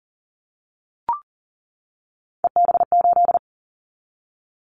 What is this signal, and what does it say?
Morse code, heard as a keyed tone, about 40 wpm, 700 Hz. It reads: EB8
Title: Callsign Prefixes for Top 50 DXCC - 40wpm